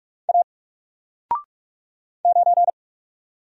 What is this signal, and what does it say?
Morse code, heard as a keyed tone, about 44 wpm, 700 Hz. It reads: A 9
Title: Single Letter-Number 45wpm